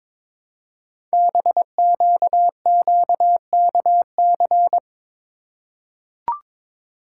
Morse code, heard as a keyed tone, about 22 wpm, 700 Hz. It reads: BQQKC
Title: Code Groups - 5 Letters - 22wpm